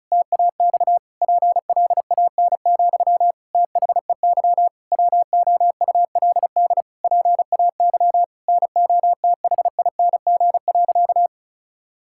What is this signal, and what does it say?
Morse code, heard as a keyed tone, about 36 wpm, 700 Hz. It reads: TAX PLAN, THEY WOULD PAY NOTHING.
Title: Sentences from Top 1000 Words - Review Part 5 - 35wpm